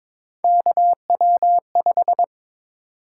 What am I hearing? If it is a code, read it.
Morse code, heard as a keyed tone, about 22 wpm, 700 Hz. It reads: KW5
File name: practice / 22wpm / Three Letter-Number Combinations - 22wpm